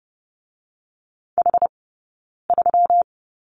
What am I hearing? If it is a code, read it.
Morse code, heard as a keyed tone, about 30 wpm, 700 Hz. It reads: H 3
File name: Lesson 20d - Number 3 - Two Characters 30wpm